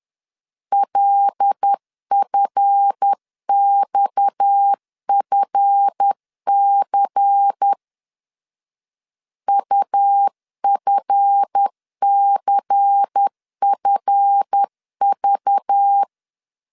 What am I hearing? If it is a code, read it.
Morse code, heard as a keyed tone, about 10 wpm, 800 Hz. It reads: LFXFC UFCFV
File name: LEZIONI / 7 GF9Z / 7 GF 10wpm 2514